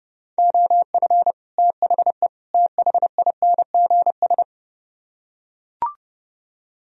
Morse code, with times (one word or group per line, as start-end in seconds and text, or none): 0.4-1.3 OF
1.6-2.3 THE
2.5-4.4 THINGS